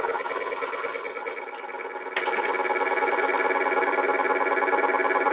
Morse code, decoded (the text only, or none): none